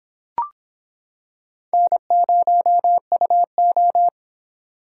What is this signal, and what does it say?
Morse code, heard as a keyed tone, about 25 wpm, 700 Hz. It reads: N0UO